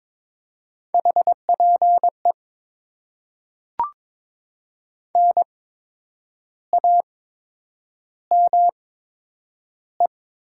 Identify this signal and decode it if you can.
Morse code, heard as a keyed tone, about 22 wpm, spaced as for 8 wpm, 700 Hz. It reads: HPE NAME
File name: Top 90 QSO Elements - 8x Character Spacing - 22wpm